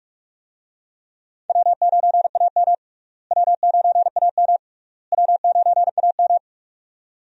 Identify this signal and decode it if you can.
Morse code, heard as a keyed tone, about 46 wpm, 700 Hz. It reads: W9AM W9AM W9AM
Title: USA Call Signs - 4 Characters - Repeated 3x - 45wpm